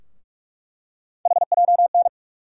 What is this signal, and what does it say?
Morse code, heard as a keyed tone, about 43 wpm, 700 Hz. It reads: HJN
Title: USA Call Sign Postfixes - 3 Letters 45wpm